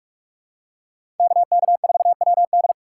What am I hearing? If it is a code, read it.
Morse code, heard as a keyed tone, about 46 wpm, 700 Hz. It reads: KK4WD
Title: USA Call Signs - 5 Characters - 4x Character Spacing - 45wpm